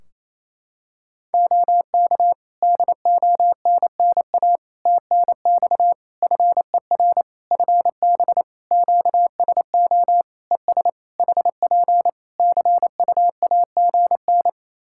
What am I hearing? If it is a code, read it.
Morse code, heard as a keyed tone, about 28 wpm, 700 Hz. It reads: OK DONNA TNX FER FB QSO ES HP CUAGN
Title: Practice QSO - Exchange 4 - Ending - 28wpm